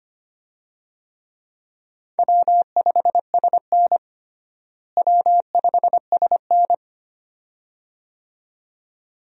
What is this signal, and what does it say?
Morse code, heard as a keyed tone, about 25 wpm, 700 Hz. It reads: W5SN W5SN
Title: USA Call Signs - 4 Characters - Repeated 2x - 25wpm